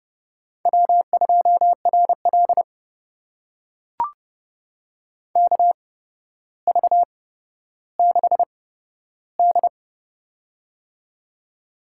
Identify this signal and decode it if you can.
Morse code, heard as a keyed tone, about 30 wpm, spaced as for 13 wpm, 700 Hz. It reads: W2RL KV6D
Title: USA Call Signs - 4 Characters - 8x Character Spacing - 30wpm